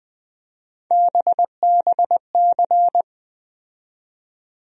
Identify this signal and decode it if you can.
Morse code, heard as a keyed tone, about 20 wpm, 700 Hz. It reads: BBC
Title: USA Call Sign Postfixes - 3 Letters 20wpm